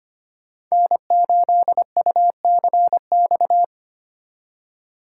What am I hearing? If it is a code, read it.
Morse code, heard as a keyed tone, about 25 wpm, 700 Hz. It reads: N8UCX
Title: USA Call Signs - 5 Characters - 2x Character Spacing - 25wpm